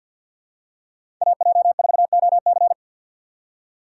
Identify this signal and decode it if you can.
Morse code, heard as a keyed tone, about 52 wpm, 700 Hz. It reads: AJ4OC
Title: USA Call Signs - 5 Characters - 2x Character Spacing - 50wpm